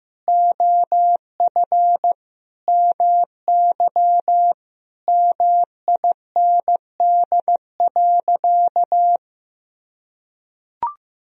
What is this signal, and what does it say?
Morse code, heard as a keyed tone, about 15 wpm, 700 Hz. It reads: OF MY MIND.
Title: Sentences from Top 1000 Words - Review Part 4 - 15wpm